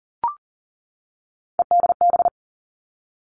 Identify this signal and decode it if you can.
Morse code, heard as a keyed tone, about 39 wpm, 690 Hz. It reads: EDB
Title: Three Challenging Letter-Number Combinations 40wpm